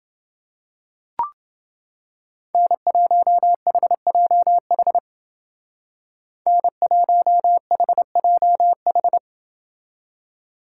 Morse code, heard as a keyed tone, about 29 wpm, 700 Hz. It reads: N1HJH N1HJH